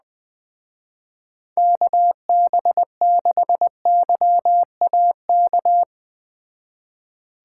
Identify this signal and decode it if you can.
Morse code, heard as a keyed tone, about 20 wpm, 700 Hz. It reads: KB6YAK
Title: Speed Racing - USA Call Signs 17wpm